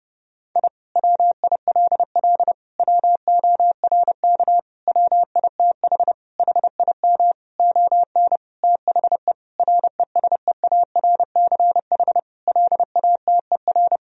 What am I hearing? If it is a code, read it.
Morse code, heard as a keyed tone, about 30 wpm, 700 Hz. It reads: I WILL WORK WITH HIM ON THE RESEARCH LATER
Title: Sentences from Top 1000 Words - Review Part 4 - 30wpm